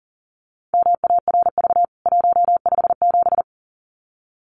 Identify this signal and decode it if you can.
Morse code, heard as a keyed tone, about 40 wpm, 700 Hz. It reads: MARV 157
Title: CWT Exchange 40wpm